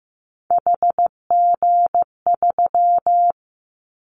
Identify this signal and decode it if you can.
Morse code, heard as a keyed tone, about 15 wpm, 700 Hz. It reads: HG3
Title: Callsign Prefixes for Top 25 DXCC - 15wpm